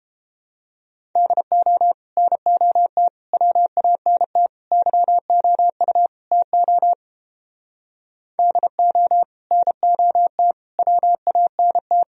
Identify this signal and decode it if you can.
Morse code, heard as a keyed tone, about 29 wpm, 700 Hz. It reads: DO NOT WANT YOU TO DO NOT WANT